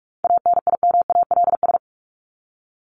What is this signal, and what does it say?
Morse code, heard as a keyed tone, about 44 wpm, 700 Hz. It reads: ANIMALS
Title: ES:S Suffix - Top 100 Words with Base Words 45wpm